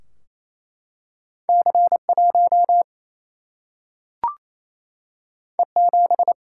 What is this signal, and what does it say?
Morse code, heard as a keyed tone, about 28 wpm, 700 Hz. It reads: C1 E7
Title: Single-Letter Single-Number 28wpm